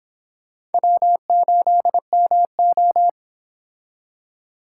Morse code, heard as a keyed tone, about 29 wpm, 700 Hz. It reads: W8MO